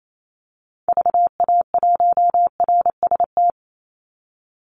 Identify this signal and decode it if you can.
Morse code, heard as a keyed tone, about 28 wpm, 700 Hz. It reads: VA1RST
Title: Call Signs Repeated 2x - 28wpm